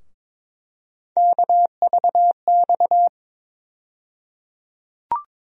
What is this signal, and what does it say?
Morse code, heard as a keyed tone, about 22 wpm, 700 Hz. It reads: KVX
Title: USA Call Sign Postfixes - 3 Letters 22wpm